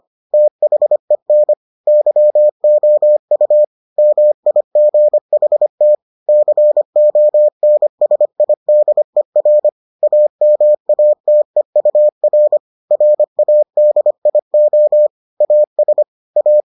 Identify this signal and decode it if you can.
Morse code, heard as a keyed tone, about 25 wpm, 600 Hz. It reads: THEN YOU MIGHT CONSIDER AMATEUR RADIO AS A